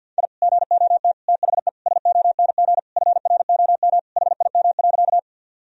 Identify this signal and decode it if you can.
Morse code, heard as a keyed tone, about 50 wpm, 700 Hz. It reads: I GOT THE SONG FROM HIM.
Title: Sentences from Top 1000 Words - Review Part 2 - 50wpm